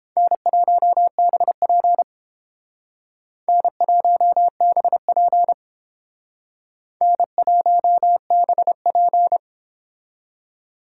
Morse code, heard as a keyed tone, about 30 wpm, 700 Hz. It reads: N1BP N1BP N1BP